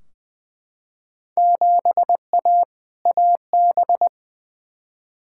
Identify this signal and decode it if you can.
Morse code, heard as a keyed tone, about 20 wpm, 700 Hz. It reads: 7A AB
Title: Field Day Exchanges Encore - 20wpm